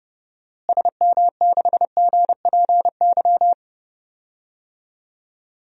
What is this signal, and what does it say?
Morse code, heard as a keyed tone, about 31 wpm, 700 Hz. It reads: SM6GPY
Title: DXCC-100 Call Signs - 6 Characters - Repeated 3x - 30wpm